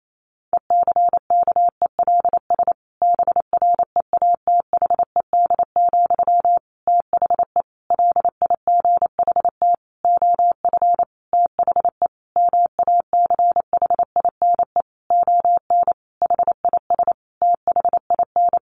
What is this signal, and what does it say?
Morse code, heard as a keyed tone, about 28 wpm, 700 Hz. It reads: ECKELS BREATHED, THE LIGHT OF THE MACHINE ON HIS THIN
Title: A Sound Of Thunder 28wpm